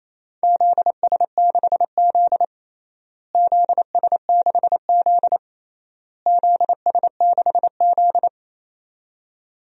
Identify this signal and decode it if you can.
Morse code, heard as a keyed tone, about 27 wpm, 700 Hz. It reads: ZS6Z ZS6Z ZS6Z
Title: DXCC-100 Call Signs - 4 Characters - Repeated 3x - 28wpm